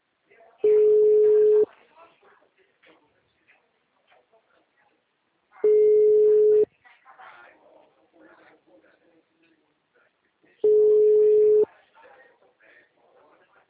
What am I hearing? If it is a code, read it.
Ringback tone (Europe, 425 Hz), with speech.